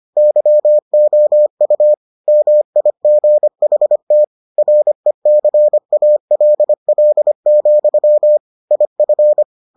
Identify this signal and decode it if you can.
Morse code, heard as a keyed tone, about 25 wpm, 600 Hz. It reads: YOU MIGHT RECALL, IF